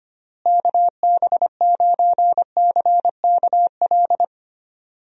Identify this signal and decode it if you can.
Morse code, heard as a keyed tone, about 25 wpm, 700 Hz. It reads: KB9CKL
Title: USA Call Signs - 6 Characters - Repeated 2x - 25wpm